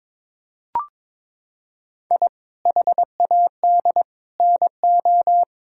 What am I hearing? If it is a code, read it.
Morse code, heard as a keyed tone, about 21 wpm, 700 Hz. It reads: I HAD NO
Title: Sentences from Top 1000 Words - Review Part 6 - 22wpm